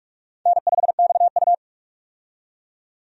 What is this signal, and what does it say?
Morse code, heard as a keyed tone, about 42 wpm, 700 Hz. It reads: N5XU
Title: USA Call Signs - 4 Characters - 45wpm